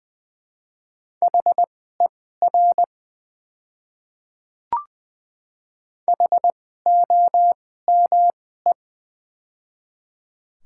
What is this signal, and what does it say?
Morse code, heard as a keyed tone, about 20 wpm, spaced as for 14 wpm, 700 Hz. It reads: HER HOME
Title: 2x Character Spacing - Top 200 Words - 20wpm